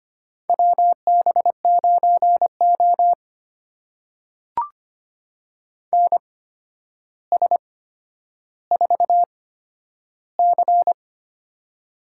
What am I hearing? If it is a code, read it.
Morse code, heard as a keyed tone, about 25 wpm, spaced as for 11 wpm, 700 Hz. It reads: WB9O NS4C
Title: USA Call Signs - 4 Characters - 8x Character Spacing - 25wpm